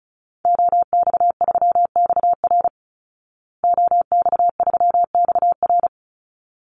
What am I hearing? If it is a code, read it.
Morse code, heard as a keyed tone, about 34 wpm, 700 Hz. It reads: OX3XR OX3XR
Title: Call Signs Repeated 3x - 35wpm